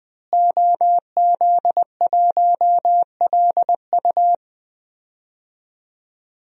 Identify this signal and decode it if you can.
Morse code, heard as a keyed tone, about 20 wpm, 700 Hz. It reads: OZ1LU
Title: DXCC-100 Call Signs - 5 Characters - Repeated 2x - 20wpm